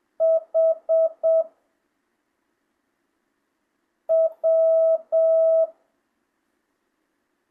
Morse code, heard as a keyed tone, about 7 wpm, 660 Hz. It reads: H W
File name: jh8cbh / 2025cw